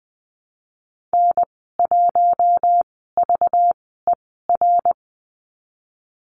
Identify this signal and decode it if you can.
Morse code, heard as a keyed tone, about 20 wpm, spaced as for 15 wpm, 700 Hz. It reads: N1VER
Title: 219 Lesson 35g - Number 0 - Call Signs 20wpm